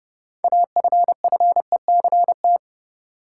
Morse code, heard as a keyed tone, about 30 wpm, 700 Hz. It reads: AFFECT
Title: Sentences from Top 900 Words - New Word Drill - 30wpm